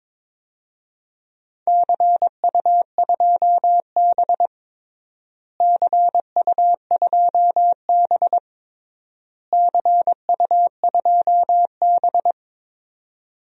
Morse code, heard as a keyed tone, about 22 wpm, 700 Hz. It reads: CU2B CU2B CU2B
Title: DXCC-100 Call Signs - 4 Characters - Repeated 3x - 22wpm